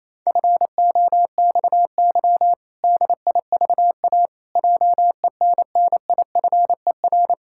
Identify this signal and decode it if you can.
Morse code, heard as a keyed tone, about 28 wpm, 700 Hz. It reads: FOXY DIVA JENNIFER
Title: Pangram Challenge v2 - 28wpm